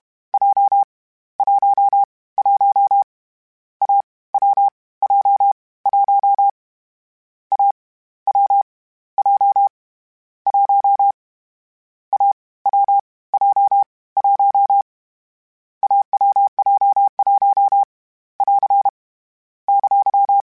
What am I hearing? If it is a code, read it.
Morse code, heard as a keyed tone, about 31 wpm, 800 Hz. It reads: J 1 1 A W J 1 A W J 1 A W J 1 AWJ1 <AR> !
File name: Difficult Combos / Difficult Combos-mp3s / AWJ1 @30